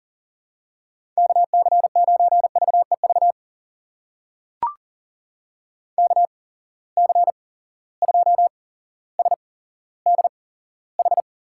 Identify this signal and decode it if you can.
Morse code, heard as a keyed tone, about 38 wpm, spaced as for 17 wpm, 700 Hz. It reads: KC9VEV KC2SDH
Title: USA Call Signs - 6 Characters - 8x Character Spacing - 40wpm